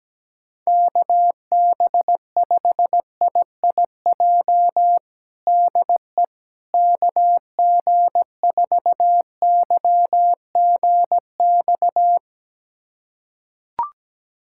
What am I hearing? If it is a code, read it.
Morse code, heard as a keyed tone, about 17 wpm, 700 Hz. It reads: KB5IIJ DE KG4YGX
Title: USA Callsign DE Callsign - 6 Characters - 17wpm